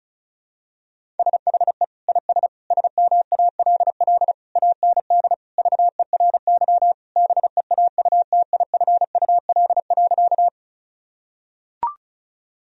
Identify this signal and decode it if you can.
Morse code, heard as a keyed tone, about 36 wpm, 700 Hz. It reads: SHE IS SMALL AND VERY BEAUTIFUL.
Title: Sentences from Top 1000 Words - Review Part 1 - 35wpm